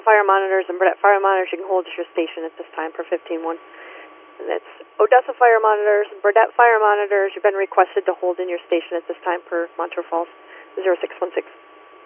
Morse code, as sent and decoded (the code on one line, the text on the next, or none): none